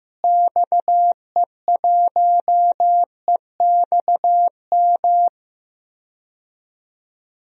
Morse code, heard as a keyed tone, about 15 wpm, 700 Hz. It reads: XE1EXM
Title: DXCC-100 Call Signs - 6 Characters - Repeated 3x - 15wpm